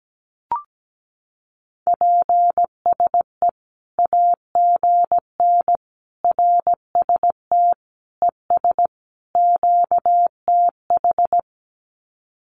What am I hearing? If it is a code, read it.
Morse code, heard as a keyed tone, about 17 wpm, 700 Hz. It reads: PSE AGN RST ES QTH
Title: Common QSO Phrases 17wpm